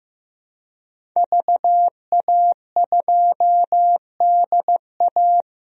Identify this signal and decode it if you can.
Morse code, heard as a keyed tone, about 15 wpm, 700 Hz. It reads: VA2DA
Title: DXCC-100 Call Signs - 5 Characters - Repeated 3x - 15wpm